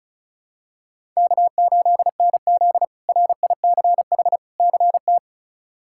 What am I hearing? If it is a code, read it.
Morse code, heard as a keyed tone, about 36 wpm, 700 Hz. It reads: K8NZ RICH CT
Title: K1USN SST QSO Practice - 35wpm